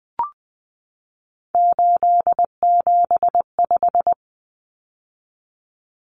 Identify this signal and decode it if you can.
Morse code, heard as a keyed tone, about 20 wpm, 700 Hz. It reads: 875
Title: Numbers - 3-Digit 20wpm